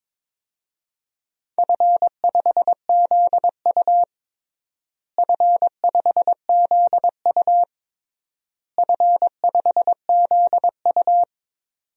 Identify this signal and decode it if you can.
Morse code, heard as a keyed tone, about 22 wpm, 700 Hz. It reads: F5ZU F5ZU F5ZU
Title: DXCC-100 Call Signs - 4 Characters - Repeated 3x - 22wpm